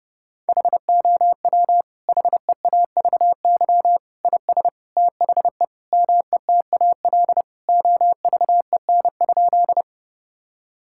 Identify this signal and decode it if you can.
Morse code, heard as a keyed tone, about 28 wpm, 700 Hz. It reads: HOW HEAVY IS THE METAL OVEN?
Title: Sentences from Top 1000 Words - Review Part 6 - 30wpm